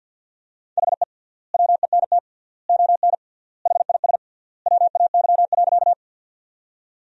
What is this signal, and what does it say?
Morse code, heard as a keyed tone, about 50 wpm, 700 Hz. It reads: HE WENT ON HIS WAY.
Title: 3x Word-Spacing - Sentences from Top 200 Words - 50wpm